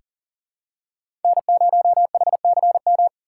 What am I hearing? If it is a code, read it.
Morse code, heard as a keyed tone, about 40 wpm, 700 Hz. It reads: N0HCM
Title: USA Call Signs - 6x Character Spacing - 40wpm